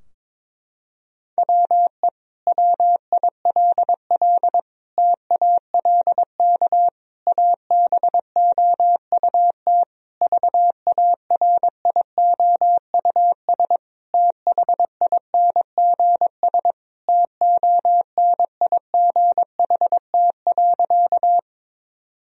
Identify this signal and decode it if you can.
Morse code, heard as a keyed tone, about 22 wpm, 700 Hz. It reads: WE WILL TALK ABOUT VARIOUS THINGS TONIGHT.
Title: Sentences from Top 1000 Words - Review Part 4 - 22wpm